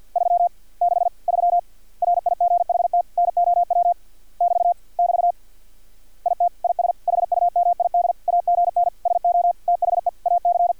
Morse code, heard as a keyed tone, about 50 wpm, 700 Hz. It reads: 363 RIGHT NOW <BT> <BT> IT IS HUMID AGN SO THE AC
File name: archived / CRSnet